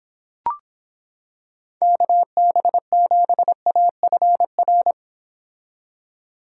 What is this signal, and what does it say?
Morse code, heard as a keyed tone, about 26 wpm, 700 Hz. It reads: KB7AFR